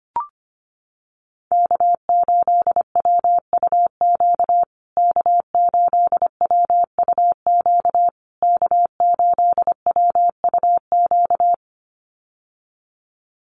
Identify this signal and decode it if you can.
Morse code, heard as a keyed tone, about 25 wpm, 700 Hz. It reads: K8WUQ K8WUQ K8WUQ
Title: USA Call Signs Repeated 3x 25wpm